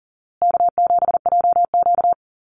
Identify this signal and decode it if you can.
Morse code, heard as a keyed tone, about 37 wpm, 700 Hz. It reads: K7JQ
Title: Easy Call Signs Repeated 3x - 40wpm